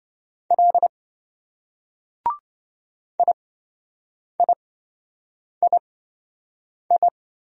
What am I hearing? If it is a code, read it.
Morse code, heard as a keyed tone, about 26 wpm, spaced as for 7 wpm, 700 Hz. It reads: L IIII